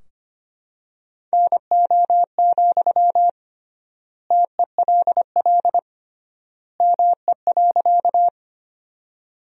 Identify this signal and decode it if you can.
Morse code, heard as a keyed tone, about 25 wpm, 700 Hz. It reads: NO, TELL ME.
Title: 3x Word-Spacing - Sentences from Top 200 Words - 25wpm